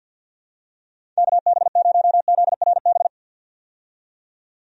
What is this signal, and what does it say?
Morse code, heard as a keyed tone, about 51 wpm, 700 Hz. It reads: KB0ZRB
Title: USA Call Signs - 6 Characters - 50wpm